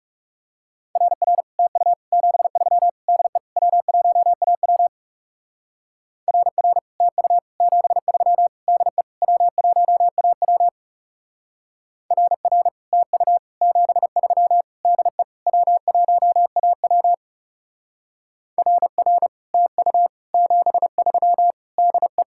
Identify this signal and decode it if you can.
Morse code, heard as a keyed tone, about 36 wpm, 700 Hz. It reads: RR TU 73 DE W1AW RR TU 73 DE W1AW RR TU 73 DE W1AW RR TU 73 DE
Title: Speed-Racing Common QSO Phrases - 30wpm